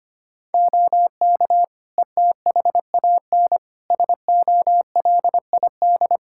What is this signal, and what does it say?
Morse code, heard as a keyed tone, about 25 wpm, 700 Hz. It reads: OK ETHAN SOLID
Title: Practice QSO - Exchange 3 - More Info - 25wpm